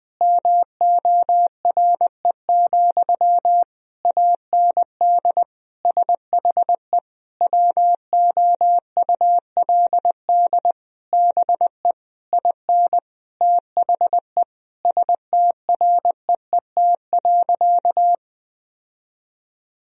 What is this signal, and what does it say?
Morse code, heard as a keyed tone, about 20 wpm, 700 Hz. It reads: MORE, AND SHE WOULD BE IN THE STREET.
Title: Sentences from Top 1000 Words - Review Part 3 - 20wpm